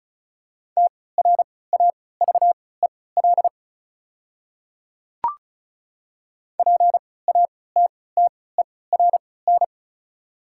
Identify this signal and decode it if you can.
Morse code, heard as a keyed tone, about 36 wpm, spaced as for 20 wpm, 690 Hz. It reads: TRAVEL PATTERN
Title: 3x Character Spacing - Top 500 Words - 35wpm